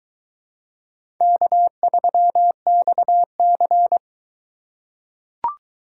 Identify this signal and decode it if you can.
Morse code, heard as a keyed tone, about 22 wpm, 700 Hz. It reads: K3XC